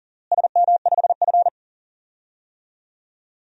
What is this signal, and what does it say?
Morse code, heard as a keyed tone, about 40 wpm, 700 Hz. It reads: SM5F
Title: DXCC-100 Call Signs - 4 Characters - Repeated 2x - 40wpm